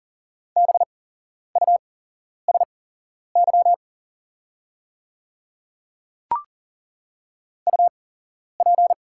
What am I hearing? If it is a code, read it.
Morse code, heard as a keyed tone, about 40 wpm, spaced as for 11 wpm, 700 Hz. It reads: BUSY UP
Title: 8x Character Spacing - Top 500 Words - 40wpm